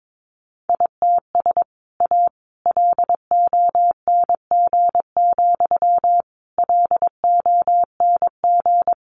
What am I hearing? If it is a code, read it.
Morse code, heard as a keyed tone, about 22 wpm, 700 Hz. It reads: ITS A LONG, LONG
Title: Sentences from Top 600 Words 22wpm